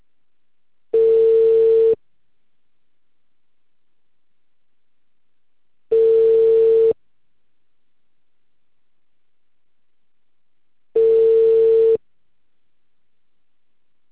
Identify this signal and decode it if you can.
Ringback tone (China, 450 Hz).
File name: out- -2044- - - .17820